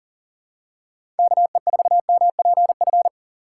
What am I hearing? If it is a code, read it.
Morse code, heard as a keyed tone, about 38 wpm, 700 Hz. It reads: KE4MPF
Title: USA Call Signs - 6 Characters - 6x Character Spacing - 40wpm